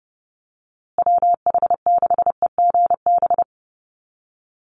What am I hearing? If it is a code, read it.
Morse code, heard as a keyed tone, about 30 wpm, 700 Hz. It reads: WH6EGB
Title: USA Call Signs Repeated 2x 30wpm